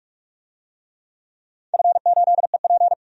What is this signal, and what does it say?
Morse code, heard as a keyed tone, about 42 wpm, 700 Hz. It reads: F8EP
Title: DXCC-100 Call Signs - 4 Characters - 45wpm